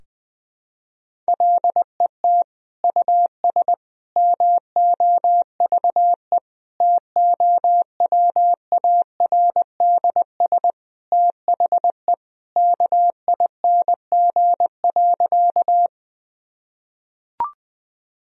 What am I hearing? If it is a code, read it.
Morse code, heard as a keyed tone, about 20 wpm, 700 Hz. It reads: LET US MOVE TOWARDS THE KING.
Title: Sentences from Top 1000 Words - Review Part 2 - 20wpm